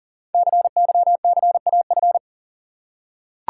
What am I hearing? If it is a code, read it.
Morse code, heard as a keyed tone, about 38 wpm, 700 Hz. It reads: CYCAF
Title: Code Groups - 5 Letters - 40wpm